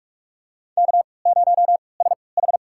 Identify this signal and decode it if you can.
Morse code, heard as a keyed tone, about 46 wpm, spaced as for 25 wpm, 700 Hz. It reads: K0SH
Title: USA Call Signs - 3x Character Spacing - 45wpm